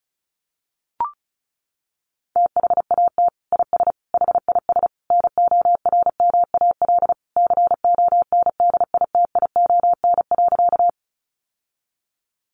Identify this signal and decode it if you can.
Morse code, heard as a keyed tone, about 34 wpm, 700 Hz. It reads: THAT IS HIS NORMAL CONDITION.
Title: Sentences from Top 1200 Words 35wpm